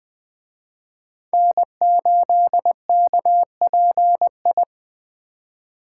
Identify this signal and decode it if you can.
Morse code, heard as a keyed tone, about 20 wpm, 700 Hz. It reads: N8KPI
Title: Speed Racing - USA 1x3 Call Signs 17wpm